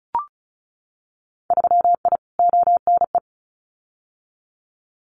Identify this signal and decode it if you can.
Morse code, heard as a keyed tone, about 36 wpm, 700 Hz. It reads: 3I ONE
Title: Winter Field Day Exchanges Encore 35wpm